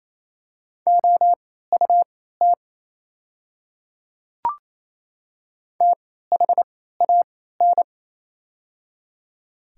Morse code, heard as a keyed tone, about 28 wpm, spaced as for 16 wpm, 700 Hz. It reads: OUT THAN
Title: 3x Character Spacing - Top 100 Words - 28wpm